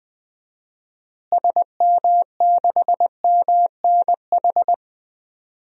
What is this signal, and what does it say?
Morse code, heard as a keyed tone, about 20 wpm, 700 Hz. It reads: SM6MNH
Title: DXCC-100 Call Signs - 6 Characters - Repeated 2x - 20wpm